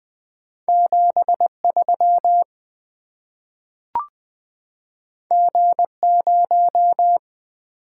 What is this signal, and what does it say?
Morse code, heard as a keyed tone, about 20 wpm, 700 Hz. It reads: 73 G0
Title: Two Letter-Number Combinations - 20wpm